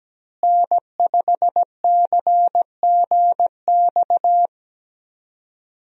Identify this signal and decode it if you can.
Morse code, heard as a keyed tone, about 17 wpm, 700 Hz. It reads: N5CGX
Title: USA Call Signs - 5 Characters - Repeated 3x - 17wpm